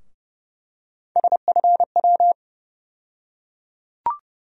Morse code, heard as a keyed tone, about 30 wpm, 700 Hz. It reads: SFW
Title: USA Call Sign Postfixes - 3 Letters 30wpm